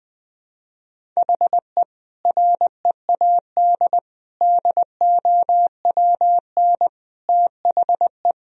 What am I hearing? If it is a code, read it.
Morse code, heard as a keyed tone, about 20 wpm, 700 Hz. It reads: HE READ DOWN THE
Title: Sentences from Top 1000 Words - Review Part 1 - 20wpm